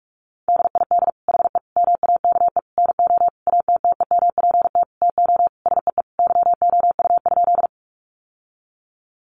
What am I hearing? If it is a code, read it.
Morse code, heard as a keyed tone, about 44 wpm, 700 Hz. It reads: DID HE MAKE NO ATTEMPT TO SEE YOU?
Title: Sentences from Top 1000 Words 45wpm